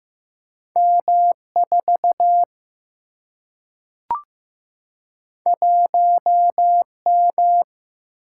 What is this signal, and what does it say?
Morse code, heard as a keyed tone, about 15 wpm, 700 Hz. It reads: M 4 1 M
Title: Single-Letter Single-Number 15wpm